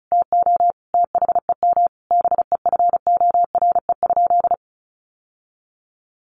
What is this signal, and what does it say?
Morse code, heard as a keyed tone, about 34 wpm, 700 Hz. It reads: TO THEM BEFORE?
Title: Sentences from Top 900 Words 35wpm